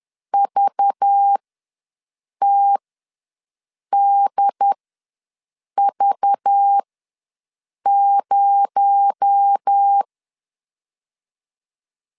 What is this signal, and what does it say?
Morse code, heard as a keyed tone, about 10 wpm, spaced as for 6 wpm, 800 Hz. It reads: VTDV0
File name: LEZIONI / 7 GF9Z / 7 GF9Z ws 10wpm 3919